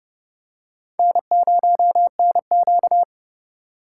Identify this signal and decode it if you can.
Morse code, heard as a keyed tone, about 29 wpm, 700 Hz. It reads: N0NQ